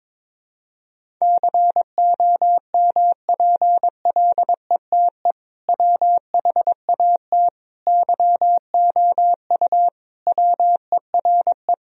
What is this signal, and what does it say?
Morse code, heard as a keyed tone, about 22 wpm, 700 Hz. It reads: COMPLETE WHAT YOU WERE
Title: Sentences from Top 1000 Words - Review Part 2 - 22wpm